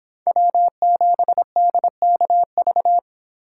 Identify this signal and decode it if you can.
Morse code, heard as a keyed tone, about 30 wpm, 700 Hz. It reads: W7DKV